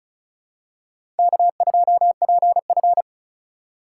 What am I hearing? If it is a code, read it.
Morse code, heard as a keyed tone, about 36 wpm, 700 Hz. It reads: K2PF
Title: USA Call Signs - 2x Character Spacing - 35wpm